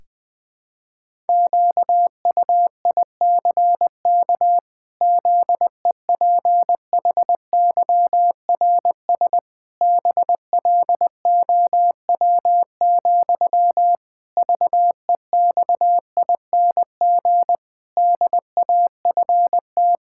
Morse code, heard as a keyed tone, about 20 wpm, 700 Hz. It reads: QUICK ZEPHYRS BLOW, VEXING DAFT
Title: Pangram Challenge v2 - 20wpm